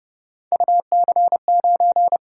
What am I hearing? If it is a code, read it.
Morse code, heard as a keyed tone, about 31 wpm, 700 Hz. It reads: UC9
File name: Speed-Racing - Three Letter-Number Combinations - 25wpm